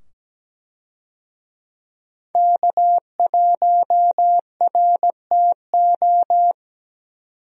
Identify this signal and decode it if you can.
Morse code, heard as a keyed tone, about 17 wpm, 700 Hz. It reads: K1RTO
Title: USA Call Signs - 5 Characters - 17wpm